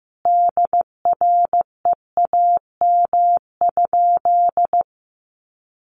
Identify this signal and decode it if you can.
Morse code, heard as a keyed tone, about 15 wpm, 700 Hz. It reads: DREAM?
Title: Sentences from Top 1500 Words 15wpm